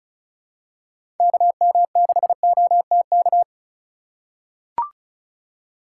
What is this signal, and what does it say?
Morse code, heard as a keyed tone, about 34 wpm, 700 Hz. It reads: KM6OTK
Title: USA Call Signs - 2x Character Spacing - 35wpm